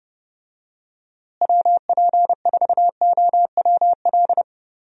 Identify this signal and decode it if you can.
Morse code, heard as a keyed tone, about 30 wpm, 700 Hz. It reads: WP4OWL
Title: USA Call Signs - 6 Characters - Repeated 2x - 30wpm